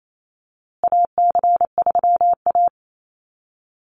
Morse code, heard as a keyed tone, about 28 wpm, 700 Hz. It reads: AC3A
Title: USA Call Signs Repeated 3x 28wpm